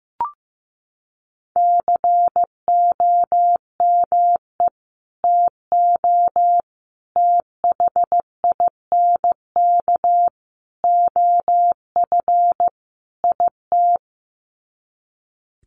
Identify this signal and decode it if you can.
Morse code, heard as a keyed tone, about 15 wpm, 700 Hz. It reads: COME TO THINK OF IT
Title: Sets of 5 Words Encore 15wpm